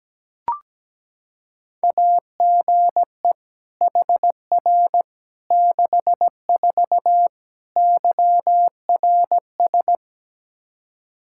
Morse code, heard as a keyed tone, about 17 wpm, 700 Hz. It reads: AGE HR 64 YRS
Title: K1USN SST QSO Practice 17wpm